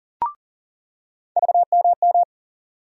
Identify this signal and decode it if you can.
Morse code, heard as a keyed tone, about 40 wpm, 700 Hz. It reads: VMM
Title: USA Call Sign Postfixes - 3 Letters 40wpm